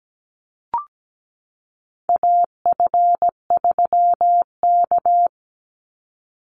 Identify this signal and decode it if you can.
Morse code, heard as a keyed tone, about 17 wpm, 700 Hz. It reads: AF3K
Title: Call Signs Repeated 2x - 17wpm